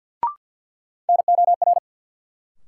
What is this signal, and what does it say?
Morse code, heard as a keyed tone, about 51 wpm, 700 Hz. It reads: NOR
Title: Sentences from Top 1500 Words - New Word Drill - 50wpm